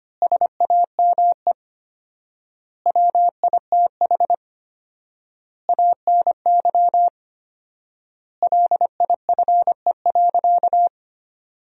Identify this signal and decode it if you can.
Morse code, heard as a keyed tone, about 25 wpm, 700 Hz. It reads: SAME WITH ANY LIFE.
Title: 4x Word-Spacing - Sentences from Top 300 Words - 25wpm